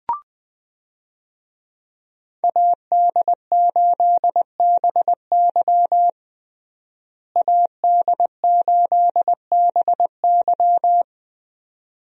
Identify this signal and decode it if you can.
Morse code, heard as a keyed tone, about 21 wpm, 700 Hz. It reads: AD8BY AD8BY
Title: USA Call Signs - 5 Characters - Repeated 2x - 20wpm